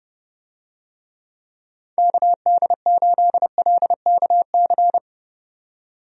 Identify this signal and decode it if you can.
Morse code, heard as a keyed tone, about 30 wpm, 700 Hz. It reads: KD8LKC
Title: USA Call Signs - 6 Characters - 30wpm